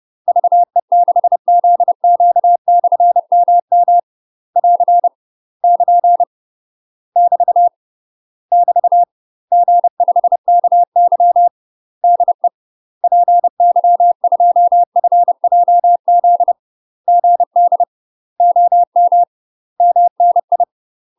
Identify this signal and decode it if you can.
Morse code, heard as a keyed tone, about 30 wpm, 700 Hz. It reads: VE6ZQ/MM <AR> <KN> <BT> <BT> G5KY DE PY2FJZ GD OM MNI
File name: UK MASTER QSOs TRAINING/30WPM